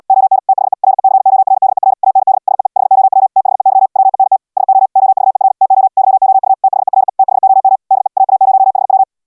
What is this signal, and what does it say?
Morse code, heard as a keyed tone, about 64 wpm, 780 Hz. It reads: ON THE RECORDER AND SEE HOW THEY WENT SO OVER TO YOU THERE THOM DE AA0HW